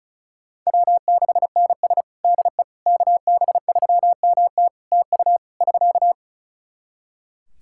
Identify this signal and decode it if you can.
Morse code, heard as a keyed tone, about 36 wpm, 700 Hz. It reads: W6NS DE KB3MT TU <SK>
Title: Practice QSO - Exchange 4 - Ending - 35wpm